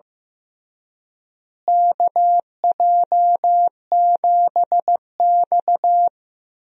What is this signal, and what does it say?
Morse code, heard as a keyed tone, about 15 wpm, 700 Hz. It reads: KJ7X
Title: USA Call Signs - 4 Characters - Repeated 3x - 15wpm